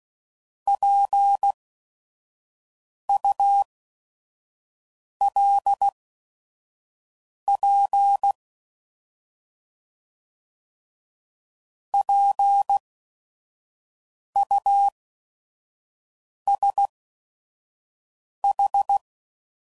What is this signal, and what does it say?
Morse code, heard as a keyed tone, about 16 wpm, spaced as for 5 wpm, 800 Hz. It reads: PULP PUSH
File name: Prefixes / By First Letter / Ps